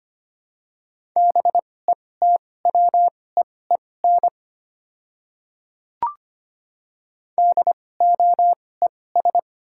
Morse code, heard as a keyed tone, about 25 wpm, spaced as for 18 wpm, 700 Hz. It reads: BETWEEN DOES
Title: 2x Character Spacing - Top 300 Words - 25wpm